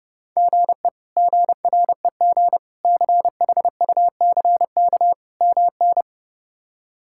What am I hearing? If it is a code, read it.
Morse code, heard as a keyed tone, about 30 wpm, 700 Hz. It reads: GE GREG CHUCK MN
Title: K1USN SST QSO Practice - 30wpm